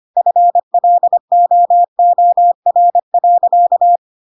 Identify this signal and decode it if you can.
Morse code, heard as a keyed tone, about 25 wpm, 700 Hz. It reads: FLOOR.
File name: Pinocchio / Pinocchio ch1 25 25 ik1hge